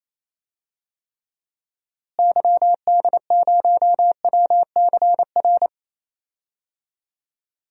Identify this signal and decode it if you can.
Morse code, heard as a keyed tone, about 28 wpm, 700 Hz. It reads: YD0WCR
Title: DXCC-100 Call Signs - 6 Characters - Repeated 3x - 28wpm